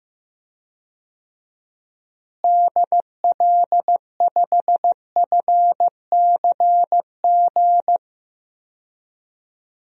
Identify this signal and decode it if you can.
Morse code, heard as a keyed tone, about 15 wpm, 700 Hz. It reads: DL5FCG
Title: DXCC-100 Call Signs - 6 Characters - Repeated 2x - 15wpm